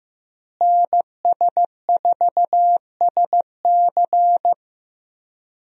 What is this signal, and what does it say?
Morse code, heard as a keyed tone, about 15 wpm, 700 Hz. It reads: NS4SC
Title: USA Call Signs - 5 Characters - Repeated 2x - 15wpm